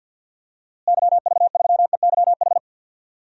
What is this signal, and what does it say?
Morse code, heard as a keyed tone, about 47 wpm, 700 Hz. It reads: YV3EYH
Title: DXCC-100 Call Signs - 6 Characters - Repeated 2x - 50wpm